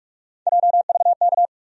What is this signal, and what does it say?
Morse code, heard as a keyed tone, about 46 wpm, 700 Hz. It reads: JVK
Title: USA Call Sign Postfixes - 3 Letters 45wpm